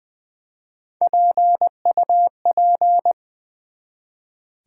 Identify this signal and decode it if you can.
Morse code, heard as a keyed tone, about 20 wpm, 700 Hz. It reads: PUP
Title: Build A Word - 1 Letter At A Time - 3 to 5-Letter Words - 20wpm